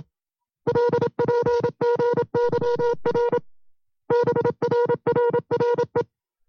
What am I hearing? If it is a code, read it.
Morse code, heard as a keyed tone, about 27 wpm, 480 Hz. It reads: LPGYR BRRRE
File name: FreeFiles / CW-practice / 25WPM-side2